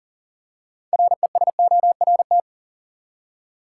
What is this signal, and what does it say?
Morse code, heard as a keyed tone, about 40 wpm, 700 Hz. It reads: RESORT
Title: Six Letter Words - 40wpm